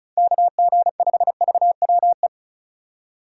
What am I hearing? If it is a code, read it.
Morse code, heard as a keyed tone, about 47 wpm, spaced as for 41 wpm, 700 Hz. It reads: KG5VWE
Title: Speed Racing - USA Call Signs 35wpm